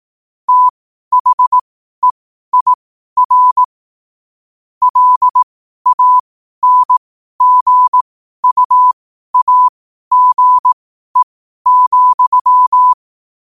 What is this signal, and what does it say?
Morse code, heard as a keyed tone, about 18 wpm, spaced as for 12 wpm, 1000 Hz. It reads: THEIR LANGUAGE,
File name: qotdfiles / 12wpm